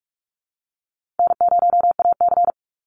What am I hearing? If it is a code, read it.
Morse code, heard as a keyed tone, about 44 wpm, 700 Hz. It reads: N0AC
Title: Easy Call Signs Repeated 2x - 45wpm